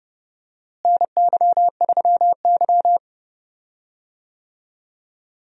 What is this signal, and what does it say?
Morse code, heard as a keyed tone, about 31 wpm, 700 Hz. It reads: NY3Y
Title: USA Call Signs - 4 Characters - Repeated 3x - 30wpm